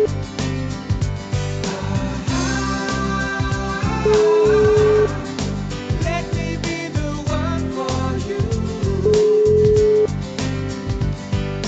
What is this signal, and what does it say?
Ringback tone (Europe, 425 Hz), with noise.